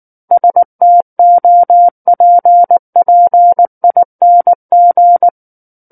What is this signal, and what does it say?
Morse code, heard as a keyed tone, about 19 wpm, 700 Hz. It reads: STOPPING
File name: Book One / 10-Chapter 10 - 19 WPM